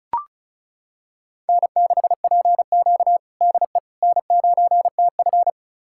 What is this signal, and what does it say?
Morse code, heard as a keyed tone, about 36 wpm, 700 Hz. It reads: N6PQ DE N9TF
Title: USA Callsign DE Callsign - 4 Characters - 35wpm